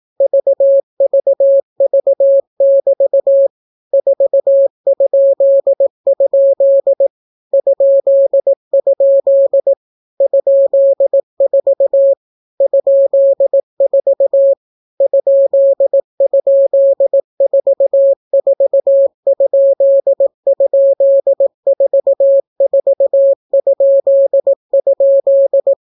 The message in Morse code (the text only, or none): VVV<BT> 4?? ?? ?4 ?4 ??44??44??